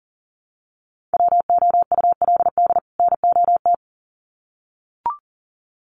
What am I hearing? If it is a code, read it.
Morse code, heard as a keyed tone, about 40 wpm, 700 Hz. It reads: WOULD NOT
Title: Sets of 2 Words 40wpm